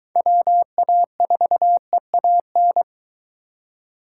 Morse code, heard as a keyed tone, about 20 wpm, 700 Hz. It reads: WA4EAN